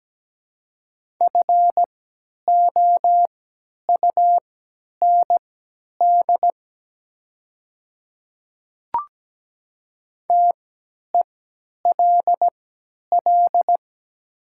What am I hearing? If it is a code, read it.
Morse code, heard as a keyed tone, about 17 wpm, spaced as for 10 wpm, 700 Hz. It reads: FOUND TELL
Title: 3x Character Spacing - Top 300 Words - 17wpm